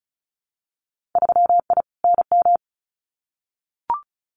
Morse code, heard as a keyed tone, about 34 wpm, 700 Hz. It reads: 3I NM
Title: Winter Field Day Exchanges Encore 35wpm